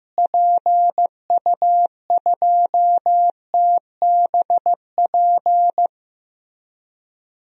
Morse code, heard as a keyed tone, about 15 wpm, 700 Hz. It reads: PU2TBP
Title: DXCC-100 Call Signs - 6 Characters - Repeated 2x - 15wpm